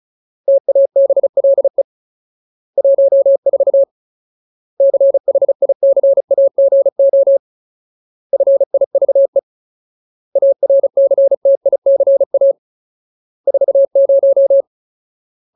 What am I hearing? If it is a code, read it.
Morse code, heard as a keyed tone, about 34 wpm, 550 Hz. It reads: TABLE 14 CHICAGO FIVE ARCTICA 40
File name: Ultimate Word List / 35WPM Ultimate